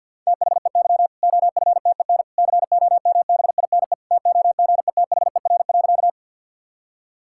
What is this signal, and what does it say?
Morse code, heard as a keyed tone, about 50 wpm, 700 Hz. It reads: THEY OFTEN COMBINE TOGETHER.
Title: Sentences from Top 1600 Words 50wpm